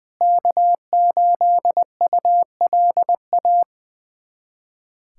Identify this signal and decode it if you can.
Morse code, heard as a keyed tone, about 20 wpm, 700 Hz. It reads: K8ULA
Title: USA Callsign DE Callsign 20wpm